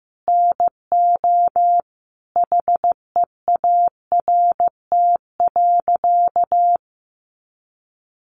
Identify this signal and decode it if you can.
Morse code, heard as a keyed tone, about 15 wpm, 700 Hz. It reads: NO HEART.
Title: Sentences from Top 1200 Words 15wpm